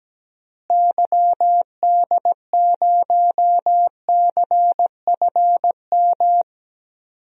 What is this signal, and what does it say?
Morse code, heard as a keyed tone, about 17 wpm, 700 Hz. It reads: YD0CFM
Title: DXCC-100 Call Signs - 6 Characters - Repeated 3x - 17wpm